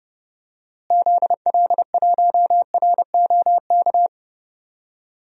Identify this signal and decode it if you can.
Morse code, heard as a keyed tone, about 30 wpm, 700 Hz. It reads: ZL1ROK
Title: DXCC-100 Call Signs - 6 Characters - Repeated 3x - 30wpm